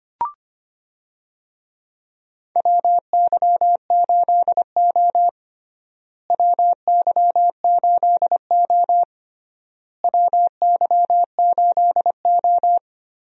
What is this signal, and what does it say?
Morse code, heard as a keyed tone, about 25 wpm, 700 Hz. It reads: WY8O WY8O WY8O
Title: USA Call Signs - 4 Characters - Repeated 3x - 25wpm